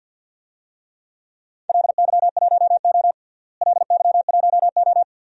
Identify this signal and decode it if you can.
Morse code, heard as a keyed tone, about 50 wpm, 700 Hz. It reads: LY1O LY1O
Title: DXCC-100 Call Signs - 4 Characters - Repeated 3x - 50wpm